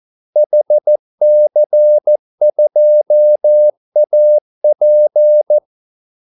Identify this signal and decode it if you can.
Morse code, heard as a keyed tone, about 14 wpm, 600 Hz. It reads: HC2AP
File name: Callsigns-14wpm